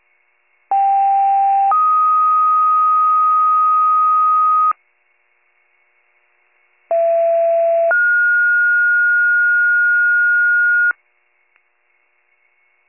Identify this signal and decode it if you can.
Two-tone page, heard as a one-second tone, then a three-second tone, about 790 then 1250 Hz, about 670 then 1430 Hz. It is heard from 0.7 to 10.9 s.